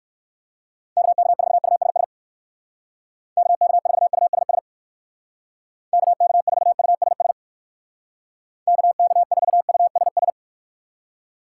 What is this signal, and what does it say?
Morse code, heard as a keyed tone, about 59 wpm, 700 Hz. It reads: KK4USS KK4USS KK4USS KK4USS